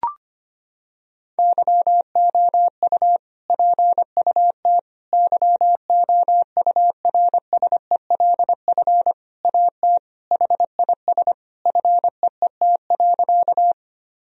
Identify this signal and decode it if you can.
Morse code, heard as a keyed tone, about 25 wpm, 700 Hz. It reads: YOU PUT YOURSELF AT HIS FEET.
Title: Sentences from Top 1000 Words - Review Part 1 - 25wpm